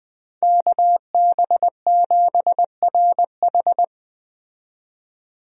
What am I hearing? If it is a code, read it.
Morse code, heard as a keyed tone, about 20 wpm, 700 Hz. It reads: KB7RH
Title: USA Call Signs - 5 Characters - 20wpm